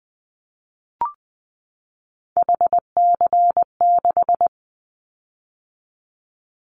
Morse code, heard as a keyed tone, about 20 wpm, 700 Hz. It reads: HC6
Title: Callsign Prefixes for Top 100 DXCC - 20wpm